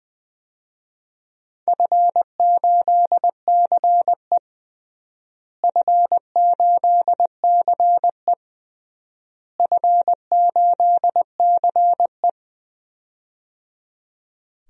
Morse code, heard as a keyed tone, about 20 wpm, 700 Hz. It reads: F8CE F8CE F8CE
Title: DXCC-100 Call Signs - 4 Characters - Repeated 3x - 20wpm